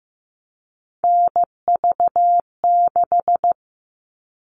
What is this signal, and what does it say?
Morse code, heard as a keyed tone, about 15 wpm, 700 Hz. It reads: NV6
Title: All Cut Numbers - 3-Digit 15wpm